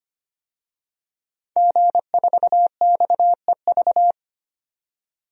Morse code, heard as a keyed tone, about 25 wpm, 700 Hz. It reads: G4XEV
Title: DXCC-100 Call Signs - 5 Characters - 25wpm